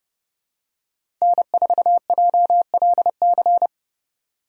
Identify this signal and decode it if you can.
Morse code, heard as a keyed tone, about 32 wpm, 700 Hz. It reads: N4JLC
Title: USA Call Signs - 5 Characters - 2x Character Spacing - 30wpm